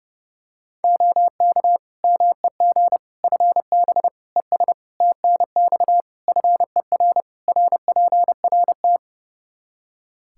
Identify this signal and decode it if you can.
Morse code, heard as a keyed tone, about 30 wpm, 700 Hz. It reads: OK MEG FB ES TNX FER RPRT
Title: Practice QSO - Exchange 2 - Info - 30wpm